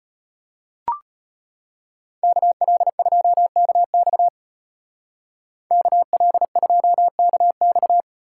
Morse code, heard as a keyed tone, about 33 wpm, 700 Hz. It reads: KL2KX KL2KX